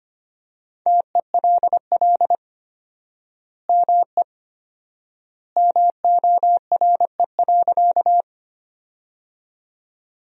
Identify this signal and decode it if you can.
Morse code, heard as a keyed tone, about 25 wpm, 700 Hz. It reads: TELL ME MORE.
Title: 4x Word-Spacing - Sentences from Top 300 Words - 25wpm